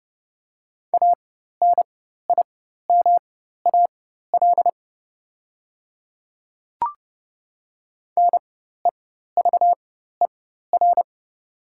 Morse code, heard as a keyed tone, about 30 wpm, spaced as for 14 wpm, 700 Hz. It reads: ANIMAL NEVER
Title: 4x Character Spacing - Top 300 Words - 30wpm